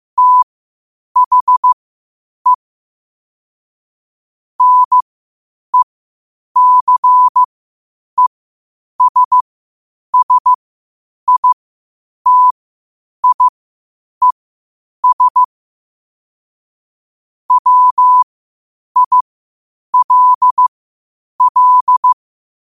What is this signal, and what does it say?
Morse code, heard as a keyed tone, about 15 wpm, spaced as for 8 wpm, 1000 Hz. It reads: THE NECESSITIES WILL
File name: qotdfiles / 8wpm